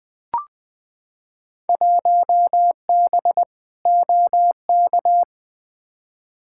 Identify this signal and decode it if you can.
Morse code, heard as a keyed tone, about 20 wpm, 700 Hz. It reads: 1B OK
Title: Field Day Exchanges Encore - 20wpm